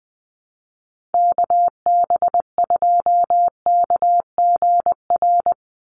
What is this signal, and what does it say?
Morse code, heard as a keyed tone, about 19 wpm, 700 Hz. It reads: KB2KGR
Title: USA Call Signs 20wpm